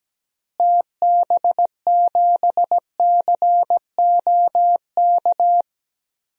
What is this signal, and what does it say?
Morse code, heard as a keyed tone, about 17 wpm, 700 Hz. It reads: TB7COK
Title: DXCC-100 Call Signs - 6 Characters - Repeated 3x - 17wpm